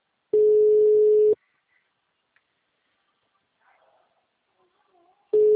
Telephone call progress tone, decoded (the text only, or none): ringback tone